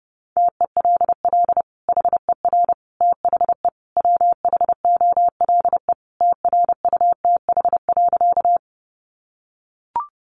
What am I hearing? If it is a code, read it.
Morse code, heard as a keyed tone, about 30 wpm, 700 Hz. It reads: TELL HER THE WHOLE TRUTH.
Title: Sentences from Top 800 Words 30wpm